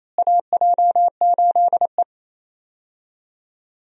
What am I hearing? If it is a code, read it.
Morse code, heard as a keyed tone, about 28 wpm, 700 Hz. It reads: AJ8E
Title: USA Call Signs - 4 Characters - Repeated 2x - 28wpm